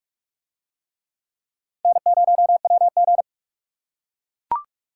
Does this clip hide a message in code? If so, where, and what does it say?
Yes: Morse code from 1.9 to 3.2 s, heard as a keyed tone, about 44 wpm, 700 Hz. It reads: N0WG